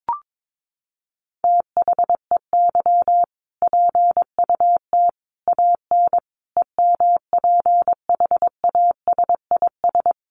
Morse code, heard as a keyed tone, about 22 wpm, 700 Hz. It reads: THEY PUT AN EMPHASIS
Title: Sentences from Top 1400 Words 22wpm